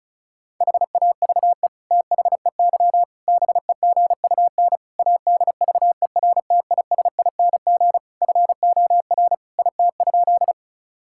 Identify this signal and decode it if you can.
Morse code, heard as a keyed tone, about 34 wpm, 700 Hz. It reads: HAVE THEY BEGUN ADVERTISING FOR IT?
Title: Sentences from Top 1000 Words - Review Part 6 - 35wpm